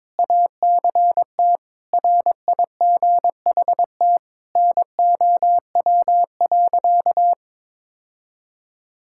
Morse code, heard as a keyed tone, about 22 wpm, 700 Hz. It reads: ACT RIGHT NOW.
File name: practice / 22wpm / Sentences with 2-Word Phrases - 22wpm